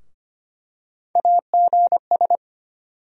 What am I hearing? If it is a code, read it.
Morse code, heard as a keyed tone, about 25 wpm, 700 Hz. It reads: AGS
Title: USA Call Sign Postfixes - 3 Letters 25wpm